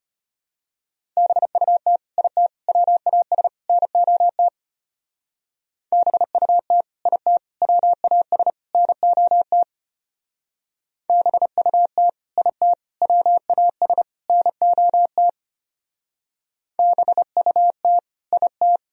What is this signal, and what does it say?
Morse code, heard as a keyed tone, about 29 wpm, 700 Hz. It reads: BUT IT WAS NOT BUT IT WAS NOT BUT IT WAS NOT BUT IT
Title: Speed-Racing - Sets of 4 Words 25wpm